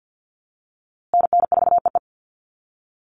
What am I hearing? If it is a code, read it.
Morse code, heard as a keyed tone, about 50 wpm, 700 Hz. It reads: NN4EE
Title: USA Call Signs Repeated 2x 50wpm